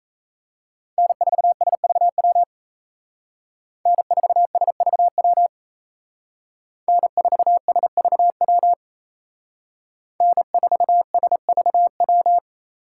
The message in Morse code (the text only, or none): N4SVW N4SVW N4SVW N4SVW